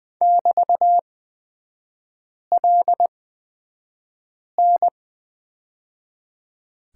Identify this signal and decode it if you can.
Morse code, heard as a keyed tone, about 20 wpm, spaced as for 5 wpm, 700 Hz. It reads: <BT>LN
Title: 224 Lesson 36e - BT - Three Characters 20wpm